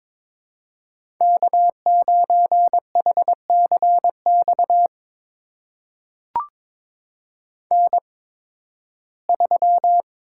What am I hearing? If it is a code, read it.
Morse code, heard as a keyed tone, about 22 wpm, spaced as for 14 wpm, 700 Hz. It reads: K9HCX N3
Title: USA Call Signs - 5 Characters - 8x Character Spacing - 22wpm